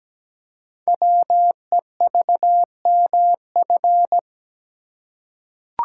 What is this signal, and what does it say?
Morse code, heard as a keyed tone, about 17 wpm, 700 Hz. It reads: WEVMF
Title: Code Groups - 5 Letters - 17wpm